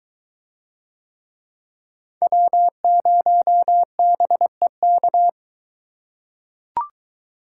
Morse code, heard as a keyed tone, about 23 wpm, 700 Hz. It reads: W0BEK